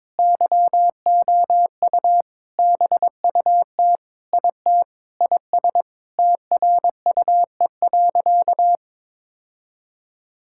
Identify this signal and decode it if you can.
Morse code, heard as a keyed tone, about 21 wpm, 700 Hz. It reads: YOU BUT IT IS TRUE.
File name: practice / 22wpm / Sentences with 8-Word Phrases - 22wpm